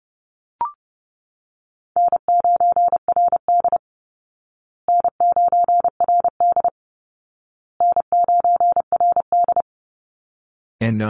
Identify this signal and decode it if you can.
Morse code, heard as a keyed tone, about 30 wpm, 700 Hz. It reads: N9RD N9RD N9RD
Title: Call Signs Repeated 2x - 30wpm